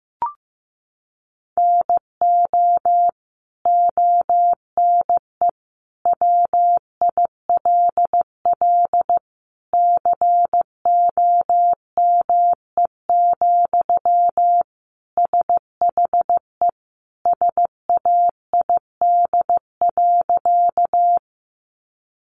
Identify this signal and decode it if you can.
Morse code, heard as a keyed tone, about 15 wpm, 700 Hz. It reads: NO ONE WILL COME, SHE SAID.
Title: Sentences from Top 100 Words 15wpm